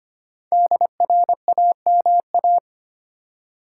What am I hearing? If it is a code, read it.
Morse code, heard as a keyed tone, about 25 wpm, 700 Hz. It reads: DRAMA
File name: 5 Letter Words - First 3 Letters Given - 25wpm